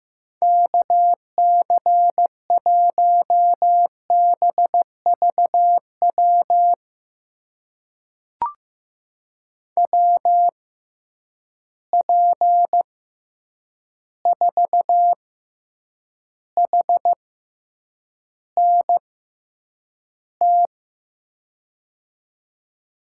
Morse code, heard as a keyed tone, about 15 wpm, spaced as for 8 wpm, 700 Hz. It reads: KC1BVW WP4HNT
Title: USA Call Signs - 6 Characters - 6x Character Spacing - 15wpm